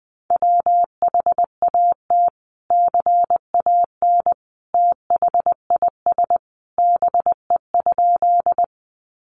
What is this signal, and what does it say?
Morse code, heard as a keyed tone, about 20 wpm, 700 Hz. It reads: WHAT CAN THIS BE?
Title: Sentences from Top 100 Words 20wpm